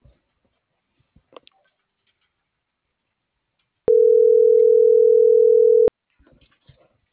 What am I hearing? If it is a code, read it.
Ringback tone (North America), 440 + 480 Hz.